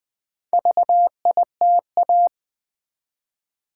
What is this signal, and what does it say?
Morse code, heard as a keyed tone, about 20 wpm, 700 Hz. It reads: VITA
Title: Build A Word - 1 Letter At A Time - 3 to 5-Letter Words - 20wpm